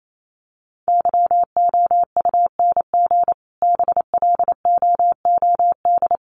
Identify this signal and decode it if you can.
Morse code, heard as a keyed tone, about 28 wpm, 700 Hz. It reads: YOUNG BLOOD
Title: Sentences from Top 1100 Words 28wpm